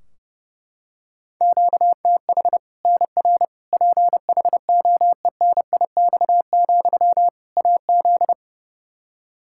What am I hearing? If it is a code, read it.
Morse code, heard as a keyed tone, about 30 wpm, 700 Hz. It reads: QTH NR PHOENIX, AZ
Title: Practice QSO - Exchange 1 - Beginning - 30wpm